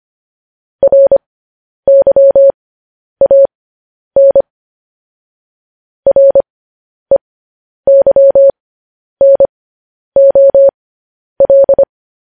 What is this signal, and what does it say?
Morse code, heard as a keyed tone, about 25 wpm, spaced as for 10 wpm, 550 Hz. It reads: RYAN REYNOL
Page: news-headlines-in-morse-code-at-10-wpm